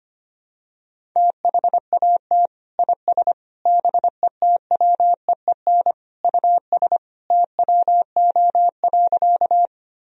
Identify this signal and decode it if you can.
Morse code, heard as a keyed tone, about 25 wpm, 700 Hz. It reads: THAT IS BETWEEN US TWO.
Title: Sentences from Top 1000 Words - Review Part 1 - 25wpm